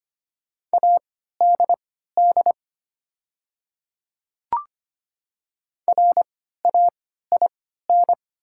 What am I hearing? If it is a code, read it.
Morse code, heard as a keyed tone, about 25 wpm, spaced as for 14 wpm, 700 Hz. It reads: ADD RAIN
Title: 3x Character Spacing - Top 500 Words - 25wpm